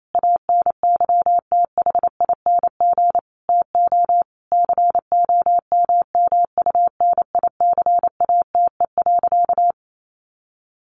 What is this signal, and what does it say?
Morse code, heard as a keyed tone, about 28 wpm, 700 Hz. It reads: ANYTHING TO COMMUNICATE.
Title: Sentences from Top 1300 Words 28wpm